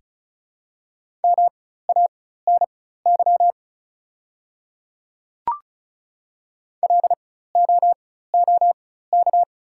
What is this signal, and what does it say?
Morse code, heard as a keyed tone, about 34 wpm, spaced as for 16 wpm, 700 Hz. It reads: MANY LOOK
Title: 4x Character Spacing - Top 100 Words - 35wpm